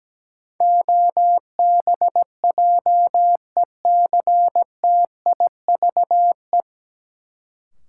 Morse code, heard as a keyed tone, about 17 wpm, 700 Hz. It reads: OBJECTIVE
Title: Sentences from Top 1200 Words - New Word Drill - 17wpm